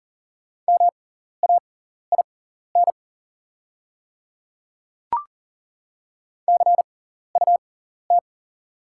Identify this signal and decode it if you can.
Morse code, heard as a keyed tone, about 41 wpm, spaced as for 14 wpm, 690 Hz. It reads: MAIN CUT
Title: 6x Character Spacing - Top 300 Words - 40wpm